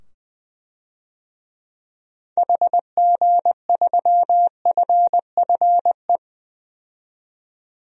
Morse code, heard as a keyed tone, about 20 wpm, 700 Hz. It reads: HG3FFE
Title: DXCC-100 Call Signs - 6 Characters - 20wpm